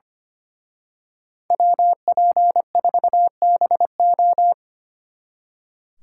Morse code, heard as a keyed tone, about 30 wpm, 700 Hz. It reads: WP4BO
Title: Speed Racing - USA Call Signs 25wpm